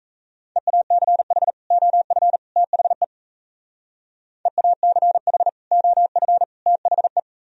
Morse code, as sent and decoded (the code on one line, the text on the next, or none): . .- -.-. .... / --- ..-. / - .... / . .- -.-. .... / --- ..-. / - .... .
EACH OF THE EACH OF THE